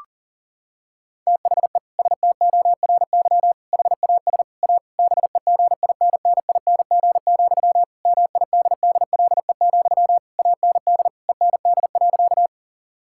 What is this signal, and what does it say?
Morse code, heard as a keyed tone, about 40 wpm, 700 Hz. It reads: THE STORY HAS A BEGINNING, MIDDLE, AND END.
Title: Sentences from Top 1000 Words - Review Part 6 - 40wpm